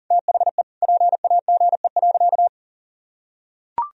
Morse code, heard as a keyed tone, about 40 wpm, 700 Hz. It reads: THE PAGE.
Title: Sentences from Top 1000 Words - Review Part 1 - 40wpm